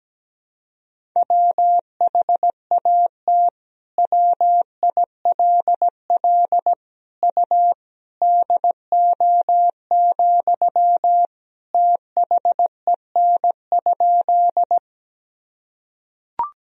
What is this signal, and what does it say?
Morse code, heard as a keyed tone, about 17 wpm, 700 Hz. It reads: WHAT WILL U DO, THEN?
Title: Sentences from Top 100 Words and Abbreviations 17wpm